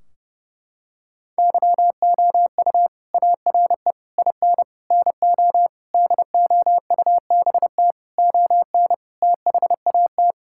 Morse code, heard as a keyed tone, about 30 wpm, 700 Hz. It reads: YOU ARE IN NO DOUBT ON THAT
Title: Sentences from Top 1600 Words 30wpm